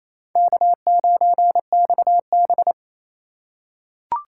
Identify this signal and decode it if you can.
Morse code, heard as a keyed tone, about 28 wpm, 700 Hz. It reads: K9XB
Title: USA Call Signs - 4 Characters - 3x Character Spacing - 28wpm